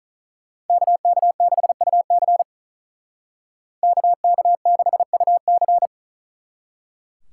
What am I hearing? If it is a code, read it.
Morse code, heard as a keyed tone, about 45 wpm, 700 Hz. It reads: KK6UC KK6UC